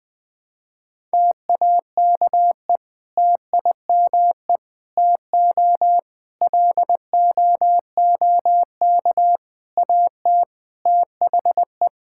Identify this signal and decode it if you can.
Morse code, heard as a keyed tone, about 19 wpm, 700 Hz. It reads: TAKE TIME TO LOOK AT THE
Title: Sentences with 4-Word Phrases - Spoken Beginning - 20wpm